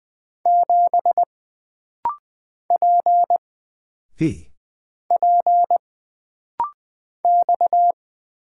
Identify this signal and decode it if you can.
Morse code, heard as a keyed tone, about 20 wpm, 700 Hz. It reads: 7 P P X
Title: Single Character (Letter:Number:Punctuation:Prosign) - Rapid-Fire - 20wpm